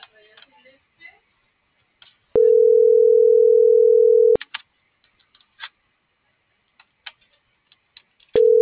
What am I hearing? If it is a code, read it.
Telephone call progress tone (North America), heard as a telephone network's tone, about 440, 480 Hz, with speech. Ringback tone.